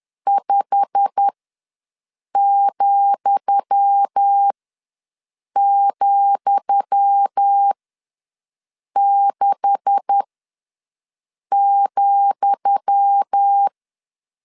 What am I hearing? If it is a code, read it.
Morse code, heard as a keyed tone, about 10 wpm, spaced as for 5 wpm, 800 Hz. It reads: 5,,6,
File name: LEZIONI / 10 SIGNS / 10D COMA ws 10wpm 4314